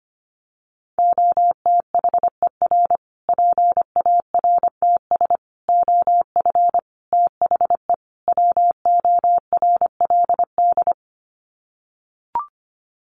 Morse code, heard as a keyed tone, about 25 wpm, 700 Hz. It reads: OTHER PARTS OF THE WORLD
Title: Sets of 5 Words Encore 25wpm